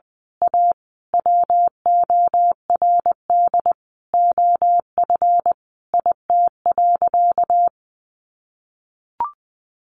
Morse code, heard as a keyed tone, about 20 wpm, 700 Hz. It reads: A WORD OF IT.
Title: Sentences from Top 1200 Words 20wpm